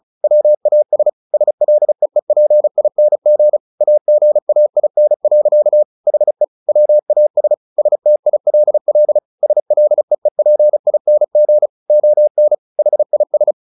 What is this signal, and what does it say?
Morse code, heard as a keyed tone, about 34 wpm, 600 Hz. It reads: WAS SLEEPING AGAIN. HE WAS STILL SLEEPING ON HIS
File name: the old man and the sea 2 35wpm